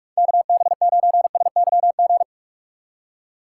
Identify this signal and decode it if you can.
Morse code, heard as a keyed tone, about 46 wpm, 700 Hz. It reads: KB9SYG
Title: USA Call Signs - 2x Character Spacing - 45wpm